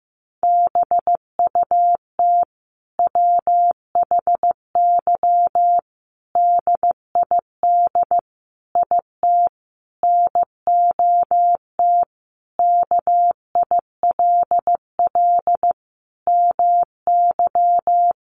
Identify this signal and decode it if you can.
Morse code, heard as a keyed tone, about 15 wpm, 700 Hz. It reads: BUT WHY DID IT NOT KILL MY
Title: Sentences from Top 1400 Words 15wpm